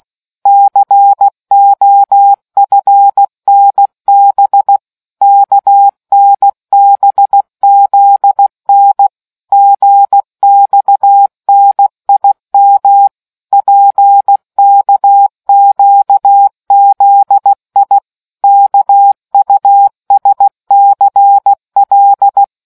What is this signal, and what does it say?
Morse code, heard as a keyed tone, about 16 wpm, 800 Hz. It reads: COFNB KNBZN GXNIM PKQZI KUSCL